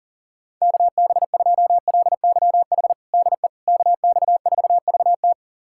Morse code, heard as a keyed tone, about 37 wpm, 700 Hz. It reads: KB2LYH DE KX4VT
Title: USA Callsign DE Callsign 40wpm